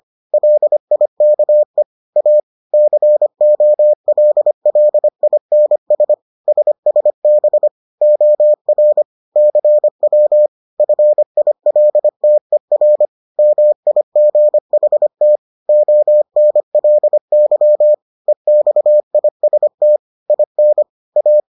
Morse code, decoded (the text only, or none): LIKE A COLLINS SSB OR CW FILTER MIGHT ONLY EXIST IN A